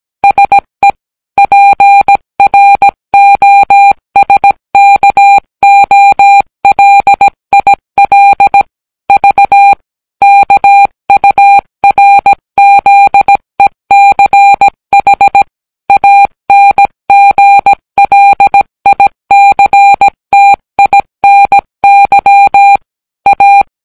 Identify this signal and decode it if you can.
Morse code, heard as a keyed tone, about 18 wpm, 800 Hz. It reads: SE PROSKOLIL V KURZECH ANGLICTINY A